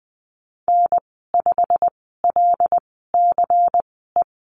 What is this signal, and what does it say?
Morse code, heard as a keyed tone, about 20 wpm, spaced as for 15 wpm, 700 Hz. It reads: N5LCE
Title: 122 Lesson 20g - Number 3 - Call Signs 20wpm